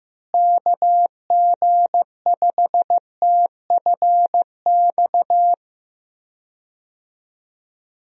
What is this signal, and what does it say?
Morse code, heard as a keyed tone, about 15 wpm, 700 Hz. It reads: KG5TFX
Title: USA Call Signs - 6 Characters - Repeated 2x - 15wpm